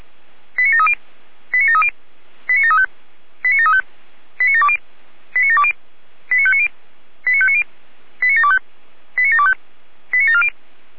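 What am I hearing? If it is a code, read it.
Five-tone selective call (ZVEI), heard as quick runs of five tones, tones of about 70 ms. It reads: 89730 89730 89735 89735 89720 89720 89590 89590 89725 89725 89740
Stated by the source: ZVEI five-tone selective call received by SDR.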